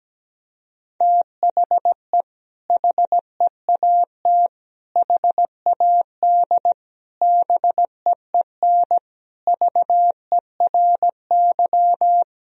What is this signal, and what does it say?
Morse code, heard as a keyed tone, about 17 wpm, 700 Hz. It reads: THE HEAT HAD BEEN VERY
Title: Sentences from Top 1000 Words - Review Part 3 - 17wpm